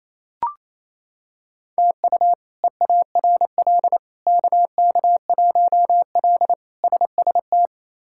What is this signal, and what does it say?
Morse code, heard as a keyed tone, about 28 wpm, 700 Hz. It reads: TU EARL KK1L SST
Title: K1USN SST QSO Practice - 28wpm